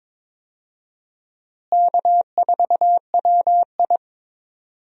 Morse code, heard as a keyed tone, about 22 wpm, 700 Hz. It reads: K4WI
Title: USA Call Signs - 4 Characters - Repeated 2x - 22wpm